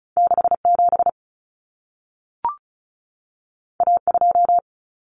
Morse code, heard as a keyed tone, about 36 wpm, 700 Hz. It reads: 67 A2
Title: Common Cut Numbers - 2-Digit 35wpm